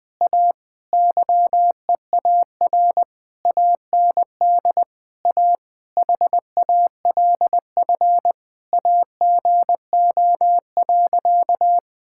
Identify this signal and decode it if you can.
Morse code, heard as a keyed tone, about 21 wpm, 700 Hz. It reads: A YEAR AND A HALF AGO.
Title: Sentences with 7-Word Phrases - Spoken Beginning - 20wpm